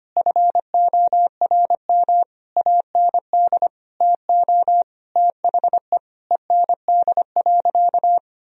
Morse code, heard as a keyed tone, about 25 wpm, 700 Hz. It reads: FORM AND TO THE END.
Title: Sentences from Top 1000 Words - Review Part 1 - 25wpm